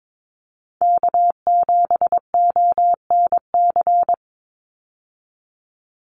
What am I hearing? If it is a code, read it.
Morse code, heard as a keyed tone, about 22 wpm, 700 Hz. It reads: K7ONC
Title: USA Call Signs 22wpm